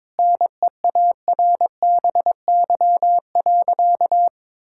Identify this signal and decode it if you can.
Morse code, heard as a keyed tone, about 22 wpm, 700 Hz. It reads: NEARBY.
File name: practice / 22wpm / Sentences with 2-Word Phrases - 22wpm